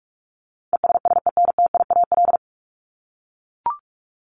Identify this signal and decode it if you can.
Morse code, heard as a keyed tone, about 44 wpm, 700 Hz. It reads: ESSENTIAL
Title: LY Suffix - Top 100 Words with Base Words 45wpm